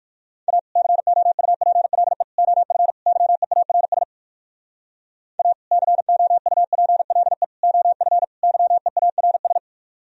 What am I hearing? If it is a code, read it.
Morse code, heard as a keyed tone, about 61 wpm, 700 Hz. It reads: A COUPLE OF YEARS A COUPLE OF YEARS